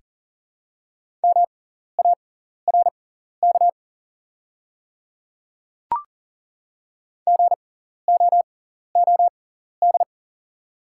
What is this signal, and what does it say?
Morse code, heard as a keyed tone, about 40 wpm, spaced as for 14 wpm, 700 Hz. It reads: MARK GOOD
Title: 6x Character Spacing - Top 300 Words - 40wpm